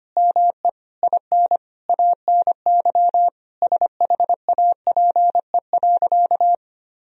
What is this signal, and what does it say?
Morse code, heard as a keyed tone, about 25 wpm, 700 Hz. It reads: ME IN ANY SHAPE.
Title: Sentences from Top 1000 Words - Review Part 3 - 25wpm